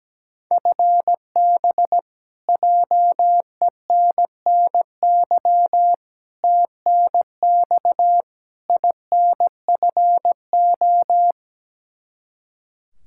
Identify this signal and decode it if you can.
Morse code, heard as a keyed tone, about 17 wpm, 700 Hz. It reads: FB JENNY TNX INFO
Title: K1USN SST QSO Practice 17wpm